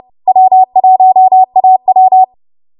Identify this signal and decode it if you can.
Morse code, heard as a keyed tone, about 30 wpm, 750 Hz. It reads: W1AW
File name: Morse / 30WPM